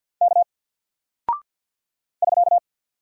Morse code, heard as a keyed tone, about 48 wpm, 700 Hz. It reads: K <SK>
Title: Single Character (Letter:Number:Punctuation:Prosign) Alt 50wpm